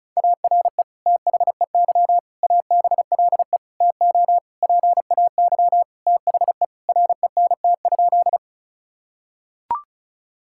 Morse code, heard as a keyed tone, about 34 wpm, 700 Hz. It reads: ARE THEY ABLE TO PAY THE RENT?
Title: Sentences from Top 1000 Words - Review Part 5 - 35wpm